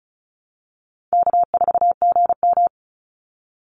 Morse code, heard as a keyed tone, about 34 wpm, 700 Hz. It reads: K4GM
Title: Easy Call Signs Repeated 2x - 35wpm